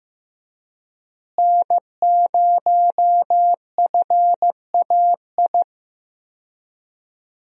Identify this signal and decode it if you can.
Morse code, heard as a keyed tone, about 16 wpm, 700 Hz. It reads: N0FAI
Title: USA Call Signs - 5 Characters - Repeated 3x - 15wpm